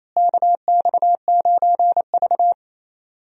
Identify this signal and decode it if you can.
Morse code, heard as a keyed tone, about 28 wpm, 700 Hz. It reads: KX9V
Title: USA Call Signs - 4 Characters - Repeated 2x - 28wpm